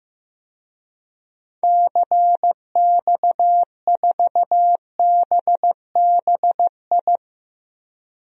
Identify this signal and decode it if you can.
Morse code, heard as a keyed tone, about 15 wpm, 700 Hz. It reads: CX4BBI
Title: DXCC-100 Call Signs - 6 Characters - Repeated 2x - 15wpm